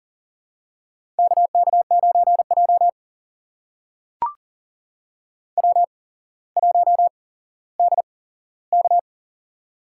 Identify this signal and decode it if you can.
Morse code, heard as a keyed tone, about 39 wpm, spaced as for 17 wpm, 690 Hz. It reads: KK9J W1DK
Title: USA Call Signs - 4 Characters - 8x Character Spacing - 40wpm